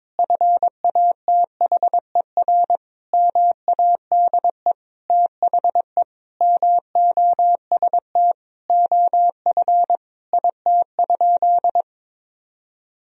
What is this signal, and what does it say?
Morse code, heard as a keyed tone, about 22 wpm, 700 Hz. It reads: FATHER MADE THE MOST OF IT?
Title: Sentences from Top 1000 Words - Review Part 1 - 22wpm